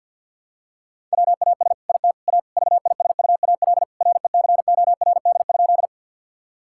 Morse code, heard as a keyed tone, about 50 wpm, 700 Hz. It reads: WAS IT A VISUAL RECORD?
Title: Sentences from Top 1000 Words - Review Part 6 - 50wpm